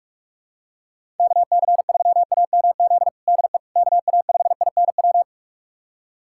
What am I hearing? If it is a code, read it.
Morse code, heard as a keyed tone, about 46 wpm, 700 Hz. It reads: KC3AMZ DE KA5INW
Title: USA Callsign DE Callsign - 6 Characters - 45wpm